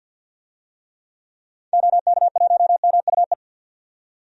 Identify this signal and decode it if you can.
Morse code, heard as a keyed tone, about 46 wpm, 700 Hz. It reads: OK1MUE
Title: DXCC-100 Call Signs - 6 Characters - 50wpm